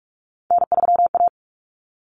Morse code, heard as a keyed tone, about 44 wpm, 700 Hz. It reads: N3A
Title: Common Cut Numbers - 3-Digit 45wpm